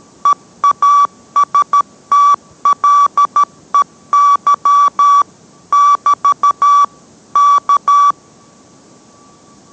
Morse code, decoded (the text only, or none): EASTLEY <BT> K